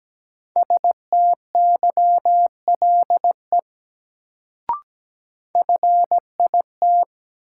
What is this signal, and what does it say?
Morse code, heard as a keyed tone, about 17 wpm, 700 Hz. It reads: STYLE FIT
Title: Sentences from Top 800 Words - New Word Drill - 17wpm